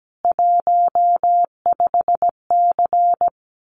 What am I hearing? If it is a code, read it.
Morse code, heard as a keyed tone, about 17 wpm, 700 Hz. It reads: 15C
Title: Common QSO Phrases 17wpm